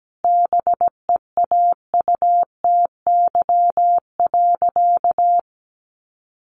Morse code, heard as a keyed tone, about 17 wpm, 700 Hz. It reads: BEAUTY.
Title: Sentences from Top 1400 Words 17wpm